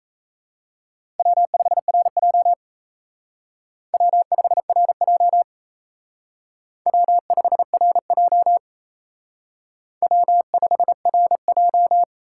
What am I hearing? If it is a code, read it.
Morse code, heard as a keyed tone, about 34 wpm, 700 Hz. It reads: W5RJ W5RJ W5RJ W5RJ
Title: Speed Racing - USA 1x2 Call Signs 28wpm